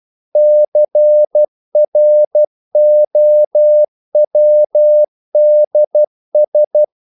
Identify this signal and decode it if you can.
Morse code, heard as a keyed tone, about 12 wpm, 600 Hz. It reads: CROWDS
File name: CW / Practice / If12